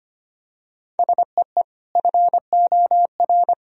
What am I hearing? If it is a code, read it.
Morse code, heard as a keyed tone, about 25 wpm, 700 Hz. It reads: SEE FOR
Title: Sentences from Top 1000 Words - Review Part 1 - 25wpm